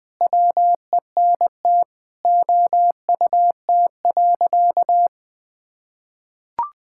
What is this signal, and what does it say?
Morse code, heard as a keyed tone, about 21 wpm, 700 Hz. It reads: WENT OUT.
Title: Sentences with 6-Word Phrases - 20wpm